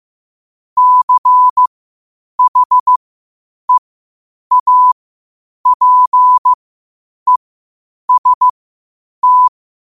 Morse code, heard as a keyed tone, about 15 wpm, spaced as for 8 wpm, 1000 Hz. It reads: CHEAPEST
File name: qotdfiles / 8wpm